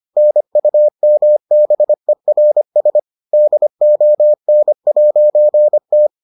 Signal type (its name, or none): Morse code